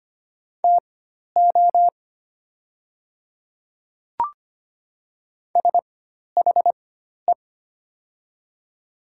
Morse code, heard as a keyed tone, about 25 wpm, spaced as for 12 wpm, 700 Hz. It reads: TO SHE
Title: 4x Character Spacing - Top 100 Words - 25wpm